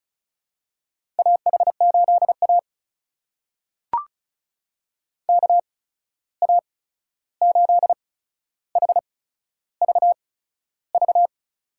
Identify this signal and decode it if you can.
Morse code, heard as a keyed tone, about 36 wpm, spaced as for 13 wpm, 700 Hz. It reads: AH8A KA8HVV
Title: USA Call Signs - 8x Character Spacing - 35wpm